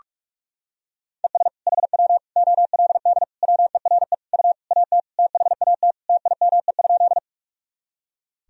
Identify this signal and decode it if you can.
Morse code, heard as a keyed tone, about 46 wpm, 700 Hz. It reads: ES HW OLD WERE U AT THAT TIME?
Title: Sentences from Top 200 Words and Abbreviations 45wpm